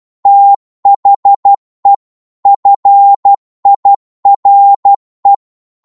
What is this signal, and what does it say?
Morse code, heard as a keyed tone, about 12 wpm, 800 Hz. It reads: THE FIRE
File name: Morse / psg-12wpm